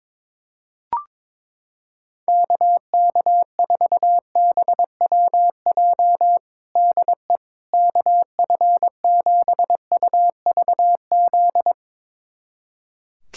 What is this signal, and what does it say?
Morse code, heard as a keyed tone, about 22 wpm, 700 Hz. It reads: KK4BWJ DE KF7UVZ
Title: USA Callsign DE Callsign - 6 Characters - 22wpm